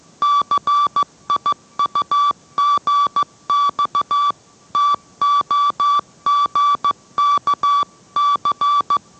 Morse code, heard as a keyed tone, about 15 wpm, 1160 Hz. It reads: CIUGX TOGKC